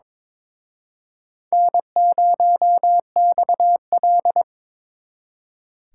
Morse code, heard as a keyed tone, about 26 wpm, 700 Hz. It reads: N0XL
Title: Speed Racing - USA Call Signs 22wpm